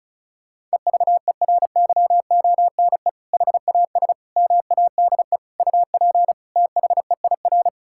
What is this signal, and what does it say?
Morse code, heard as a keyed tone, about 34 wpm, 700 Hz. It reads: EVERYONE HAS MADE UP THEIR
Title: Sentences from Top 1000 Words - Review Part 5 - 35wpm